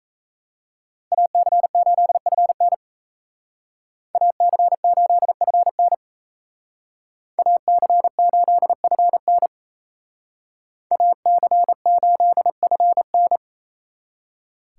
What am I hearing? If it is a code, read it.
Morse code, heard as a keyed tone, about 36 wpm, 700 Hz. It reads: AC8FN AC8FN AC8FN AC8FN